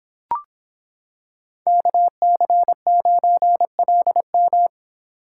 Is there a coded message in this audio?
Yes: Morse code from 1.7 to 4.7 s, heard as a keyed tone, about 26 wpm, 700 Hz. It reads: KC9LM